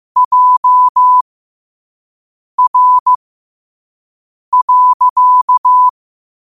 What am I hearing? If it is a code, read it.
Morse code, heard as a keyed tone, about 15 wpm, spaced as for 5 wpm, 1000 Hz. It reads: JR.
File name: qotdfiles / 5wpm